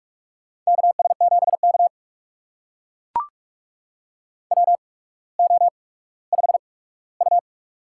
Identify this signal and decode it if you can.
Morse code, heard as a keyed tone, about 42 wpm, spaced as for 19 wpm, 700 Hz. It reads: KS7K WO5U
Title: USA Call Signs - 4 Characters - 8x Character Spacing - 45wpm